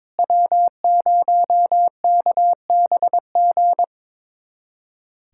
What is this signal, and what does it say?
Morse code, heard as a keyed tone, about 22 wpm, 700 Hz. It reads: W0KBG
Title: USA Callsign DE Callsign - 5 Characters - 22wpm